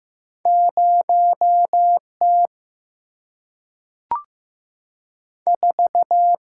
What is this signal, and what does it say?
Morse code, heard as a keyed tone, about 15 wpm, 700 Hz. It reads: 0 T 4
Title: Single-Letter Single-Number 15wpm